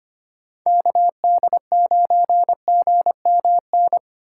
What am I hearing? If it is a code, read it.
Morse code, heard as a keyed tone, about 33 wpm, 700 Hz. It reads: KD9GMN